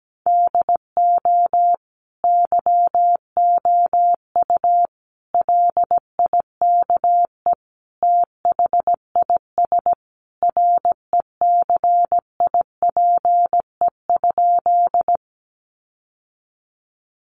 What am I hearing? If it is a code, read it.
Morse code, heard as a keyed tone, about 17 wpm, 700 Hz. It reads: DO YOU LIKE THIS RECIPE?
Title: Sentences from Top 1200 Words 17wpm